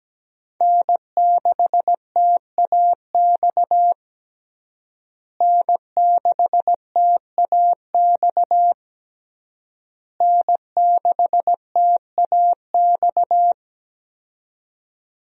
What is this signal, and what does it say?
Morse code, heard as a keyed tone, about 17 wpm, 700 Hz. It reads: N6TAX N6TAX N6TAX
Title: USA Call Signs - 5 Characters - Repeated 3x - 17wpm